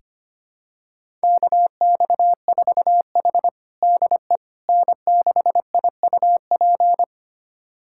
Morse code, heard as a keyed tone, about 25 wpm, 700 Hz. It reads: KX4H DE N6IUP
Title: USA Callsign DE Callsign 25wpm